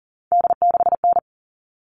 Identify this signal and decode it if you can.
Morse code, heard as a keyed tone, about 40 wpm, 700 Hz. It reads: D6N
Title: All Cut Numbers - 3-Digit 40wpm